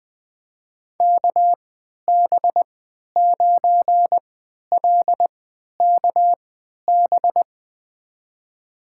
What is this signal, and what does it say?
Morse code, heard as a keyed tone, about 20 wpm, spaced as for 11 wpm, 700 Hz. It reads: KB9LKB
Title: USA Call Signs - 6 Characters - 3x Character Spacing - 20wpm